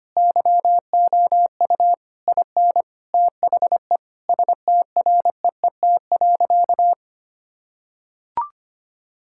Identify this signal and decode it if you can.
Morse code, heard as a keyed tone, about 25 wpm, 700 Hz. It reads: YOU IN THE STREET.
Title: Sentences from Top 1000 Words - Review Part 6 - 25wpm